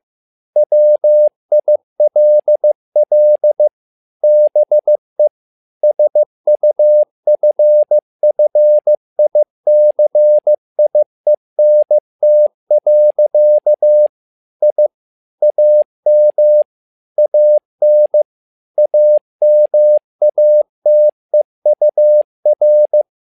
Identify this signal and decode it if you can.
Morse code, heard as a keyed tone, about 15 wpm, 600 Hz. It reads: WILL BE SUFFICIENT. I AM AN AMATEUR
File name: learning center / tower trouble / tower trouble 15wpm